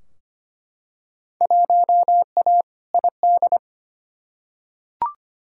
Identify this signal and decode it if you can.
Morse code, heard as a keyed tone, about 25 wpm, 700 Hz. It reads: 1A ID
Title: Field Day Exchanges - 25wpm